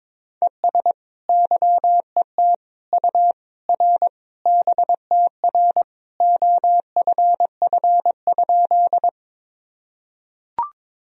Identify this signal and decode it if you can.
Morse code, heard as a keyed tone, about 22 wpm, 700 Hz. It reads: ES YET U R BTR OFF?
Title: Sentences from Top 500 Words and Abbreviations 22wpm